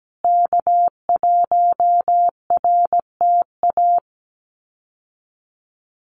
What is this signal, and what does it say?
Morse code, heard as a keyed tone, about 17 wpm, 700 Hz. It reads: K1RTA
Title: USA Call Signs Repeated 2x 17wpm